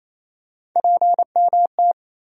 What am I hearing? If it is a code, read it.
Morse code, heard as a keyed tone, about 28 wpm, 700 Hz. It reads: PMT
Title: USA Call Sign Postfixes - 3 Letters 28wpm